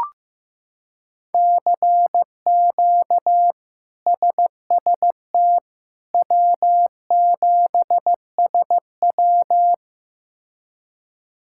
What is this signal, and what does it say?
Morse code, heard as a keyed tone, about 15 wpm, 700 Hz. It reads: CQ SST W7SW
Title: K1USN SST QSO Practice - 15wpm